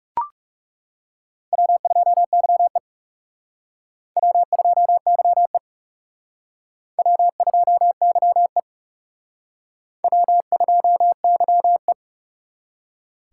Morse code, heard as a keyed tone, about 36 wpm, 700 Hz. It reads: W2YE W2YE W2YE W2YE
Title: Speed Racing - USA Call Signs 30wpm